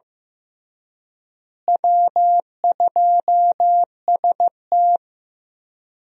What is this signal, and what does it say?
Morse code, heard as a keyed tone, about 15 wpm, 700 Hz. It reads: W2ST
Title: USA Call Signs - 4 Characters - Repeated 2x - 15wpm